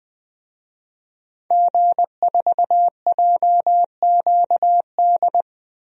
Morse code, heard as a keyed tone, about 20 wpm, 700 Hz. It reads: G4JQD
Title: DXCC-100 Call Signs - 5 Characters - 20wpm